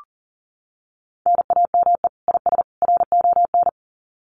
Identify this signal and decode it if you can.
Morse code, heard as a keyed tone, about 40 wpm, 700 Hz. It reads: NAME IS RON
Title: Common QSO Phrases 40wpm